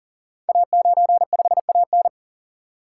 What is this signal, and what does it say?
Morse code, heard as a keyed tone, about 41 wpm, 700 Hz. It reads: A95AN
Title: Code Groups - 5 Letter:Numbers - 40wpm